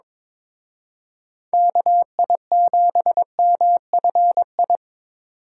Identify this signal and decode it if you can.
Morse code, heard as a keyed tone, about 25 wpm, 700 Hz. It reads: KI7MFI
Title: Speed Racing - USA 2x3 Call Signs 22wpm